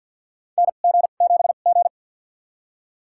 Morse code, heard as a keyed tone, about 53 wpm, spaced as for 37 wpm, 700 Hz. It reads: NG7G
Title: USA Call Signs - 4 Characters - 2x Character Spacing - 50wpm